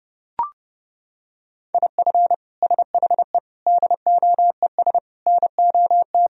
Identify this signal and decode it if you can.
Morse code, heard as a keyed tone, about 32 wpm, 700 Hz. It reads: IF SHE DOES NOT
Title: Sentences from Top 1000 Words - Review Part 3 - 30wpm